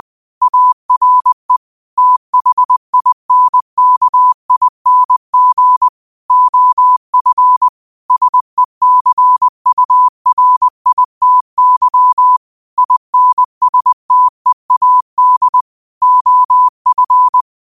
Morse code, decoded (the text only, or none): ARE THINKING OF SECURITY INSTEAD OF